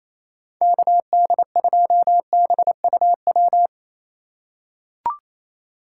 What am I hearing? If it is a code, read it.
Morse code, heard as a keyed tone, about 28 wpm, 700 Hz. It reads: KD2BUW
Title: USA Call Signs - 6 Characters - 3x Character Spacing - 28wpm